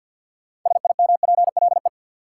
Morse code, heard as a keyed tone, about 51 wpm, 700 Hz. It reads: SIMPLE
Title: Sentences from Top 400 Words - New Word Drill - 50wpm